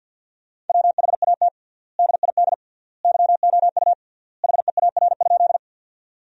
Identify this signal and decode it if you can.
Morse code, heard as a keyed tone, about 50 wpm, 700 Hz. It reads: WHAT DID YOU HEAR?
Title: 3x Word-Spacing - Sentences from Top 300 Words - 50wpm